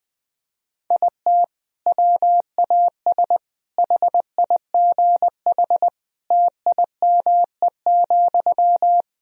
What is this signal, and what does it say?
Morse code, heard as a keyed tone, about 21 wpm, 700 Hz. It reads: IT WAS HIGH TIME,
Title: Sentences from Top 1000 Words - Review Part 1 - 20wpm